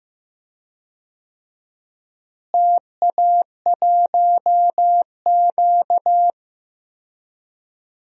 Morse code, heard as a keyed tone, about 15 wpm, 700 Hz. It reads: TA1Q
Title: DXCC-100 Call Signs - 4 Characters - Repeated 2x - 15wpm